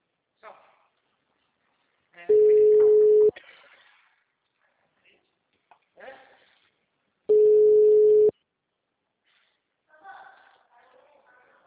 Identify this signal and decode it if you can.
Ringback tone (Europe, 425 Hz), with speech.
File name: out- -2031